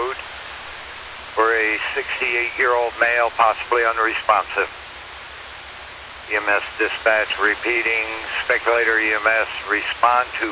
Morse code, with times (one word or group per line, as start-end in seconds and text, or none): none